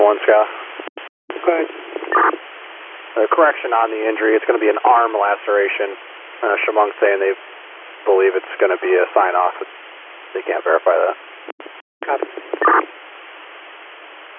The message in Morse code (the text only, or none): none